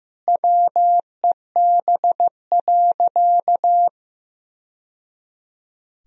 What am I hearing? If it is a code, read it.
Morse code, heard as a keyed tone, about 15 wpm, 700 Hz. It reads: WEB.
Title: Sentences from Top 1000 Words - Review Part 4 - 15wpm